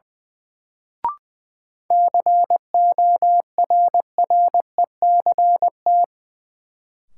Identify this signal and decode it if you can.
Morse code, heard as a keyed tone, about 20 wpm, 700 Hz. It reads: CORRECT
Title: Sentences from Top 700 Words - New Word Drill - 20wpm